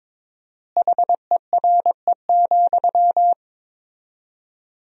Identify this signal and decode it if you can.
Morse code, heard as a keyed tone, about 22 wpm, 700 Hz. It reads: HERE,
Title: 4x Word-Spacing - Sentences from Top 300 Words - 22wpm